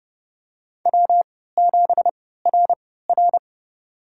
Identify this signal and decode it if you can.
Morse code, heard as a keyed tone, about 30 wpm, spaced as for 17 wpm, 700 Hz. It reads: W7RR
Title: USA Call Signs - 4 Characters - 3x Character Spacing - 30wpm